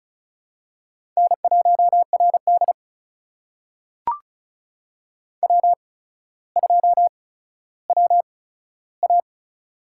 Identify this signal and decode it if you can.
Morse code, heard as a keyed tone, about 36 wpm, spaced as for 15 wpm, 700 Hz. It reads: N1RD W2WA
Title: USA Call Signs - 4 Characters - 8x Character Spacing - 35wpm